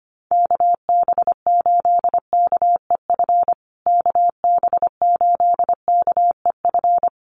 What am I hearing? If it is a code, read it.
Morse code, heard as a keyed tone, about 25 wpm, 700 Hz. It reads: KB8KEF KB8KEF
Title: USA Call Signs Repeated 3x 25wpm